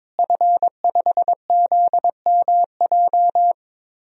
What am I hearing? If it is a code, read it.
Morse code, heard as a keyed tone, about 22 wpm, 700 Hz. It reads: F5ZMJ
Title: DXCC-100 Call Signs - 5 Characters - Repeated 2x - 22wpm